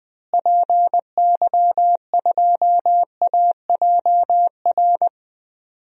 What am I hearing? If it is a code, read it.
Morse code, heard as a keyed tone, about 20 wpm, 700 Hz. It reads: PY2AJR
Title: DXCC-100 Call Signs - 6 Characters - Repeated 3x - 20wpm